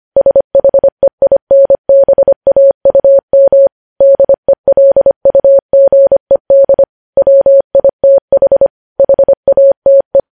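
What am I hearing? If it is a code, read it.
Morse code, heard as a keyed tone, about 25 wpm, 550 Hz. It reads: SHEINBAUM DELUGED WITH HATE
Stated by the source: Morse code news headlines sent at 25 wpm.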